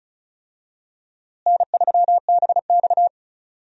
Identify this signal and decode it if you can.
Morse code, heard as a keyed tone, about 36 wpm, 700 Hz. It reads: N3BX
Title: USA Call Signs - 4 Characters - Repeated 2x - 35wpm